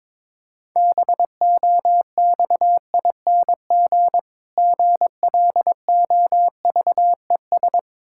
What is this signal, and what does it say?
Morse code, heard as a keyed tone, about 22 wpm, 690 Hz. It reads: BOXING GLOVES
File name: Difficult Words with Hints - 22wpm